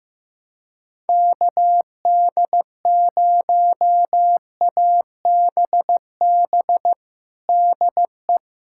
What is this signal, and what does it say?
Morse code, heard as a keyed tone, about 15 wpm, 700 Hz. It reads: KD0ABB DE
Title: USA Callsign DE Callsign - 6 Characters - 15wpm